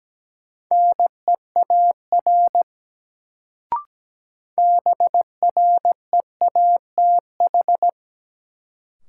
Morse code, heard as a keyed tone, about 17 wpm, 700 Hz. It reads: NEAR BREATH
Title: Sets of 5 Words Encore - New Word Drill - 17wpm